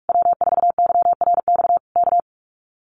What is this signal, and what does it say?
Morse code, heard as a keyed tone, about 46 wpm, 700 Hz. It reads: W4YRX K
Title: Common QSO Phrases Encore 45wpm